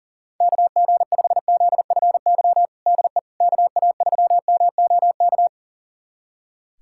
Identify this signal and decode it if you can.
Morse code, heard as a keyed tone, about 40 wpm, 700 Hz. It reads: KG5ZFY DE KA3MOK
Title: USA Callsign DE Callsign - 6 Characters - 40wpm